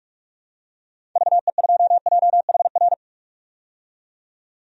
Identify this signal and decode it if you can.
Morse code, heard as a keyed tone, about 46 wpm, 700 Hz. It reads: VE2JHR
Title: DXCC-100 Call Signs - 6 Characters - 45wpm